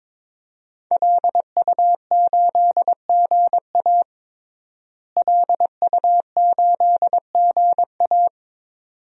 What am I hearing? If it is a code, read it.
Morse code, heard as a keyed tone, about 22 wpm, 700 Hz. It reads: LU8GA LU8GA
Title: DXCC-100 Call Signs - 5 Characters - Repeated 2x - 22wpm